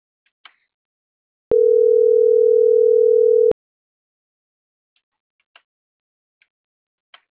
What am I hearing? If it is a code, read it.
Ringback tone (North America), 440 + 480 Hz.